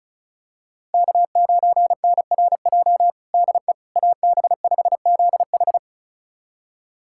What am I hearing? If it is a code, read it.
Morse code, heard as a keyed tone, about 34 wpm, 700 Hz. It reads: K9NRJ DE AB5ZH
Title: USA Callsign DE Callsign - 5 Characters - 35wpm